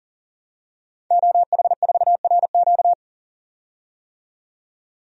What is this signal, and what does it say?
Morse code, heard as a keyed tone, about 40 wpm, 700 Hz. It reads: OH4RQ
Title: DXCC-100 Call Signs - 5 Characters - 40wpm